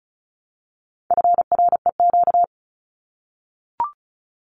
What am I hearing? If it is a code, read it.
Morse code, heard as a keyed tone, about 34 wpm, 700 Hz. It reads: FREQ
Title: Abbreviations Repeated 3x - 35wpm